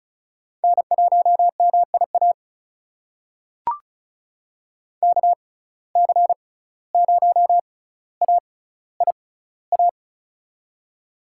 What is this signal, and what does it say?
Morse code, heard as a keyed tone, about 36 wpm, spaced as for 17 wpm, 700 Hz. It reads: N1MIA KC0AIA
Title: USA Call Signs - 6x Character Spacing - 35wpm